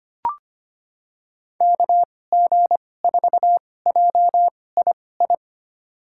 Morse code, heard as a keyed tone, about 25 wpm, spaced as for 18 wpm, 700 Hz. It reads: KG4JII
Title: USA Call Signs - 6 Characters - 2x Character Spacing - 25wpm